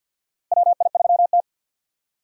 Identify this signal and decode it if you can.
Morse code, heard as a keyed tone, about 53 wpm, 700 Hz. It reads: WI3T
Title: USA Call Signs - 4 Characters - Repeated 3x - 50wpm